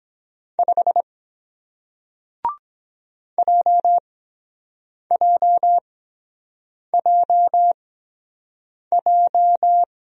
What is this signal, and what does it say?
Morse code, heard as a keyed tone, about 23 wpm, spaced as for 7 wpm, 700 Hz. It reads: 5 JJJJ